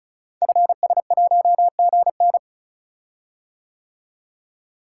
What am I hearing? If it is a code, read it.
Morse code, heard as a keyed tone, about 36 wpm, 690 Hz. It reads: FS1GN
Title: Code Groups - 5 Letter:Numbers - 35wpm